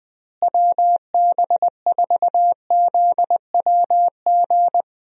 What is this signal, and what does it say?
Morse code, heard as a keyed tone, about 21 wpm, 700 Hz. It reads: WB4ZWG
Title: USA Call Signs - 6 Characters - Repeated 3x - 20wpm